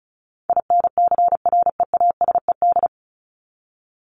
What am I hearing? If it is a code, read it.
Morse code, heard as a keyed tone, about 34 wpm, 700 Hz. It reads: INCREASED
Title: ED Suffix - Top 100 Words with Base Words 35wpm